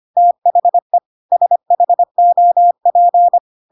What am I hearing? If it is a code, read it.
Morse code, heard as a keyed tone, about 25 wpm, 700 Hz. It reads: THE SHOP
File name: Pinocchio / Pinocchio ch1 25 25 ik1hge